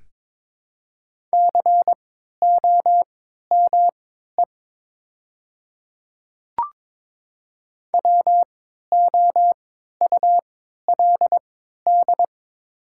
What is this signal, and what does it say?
Morse code, heard as a keyed tone, about 22 wpm, spaced as for 12 wpm, 690 Hz. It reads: COME WOULD
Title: 3x Character Spacing - Top 100 Words - 22wpm